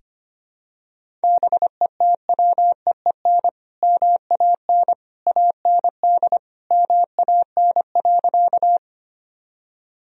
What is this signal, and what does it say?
Morse code, heard as a keyed tone, about 25 wpm, 700 Hz. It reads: BETWEEN MAN AND MAN.
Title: Sentences from Top 1000 Words - Review Part 1 - 25wpm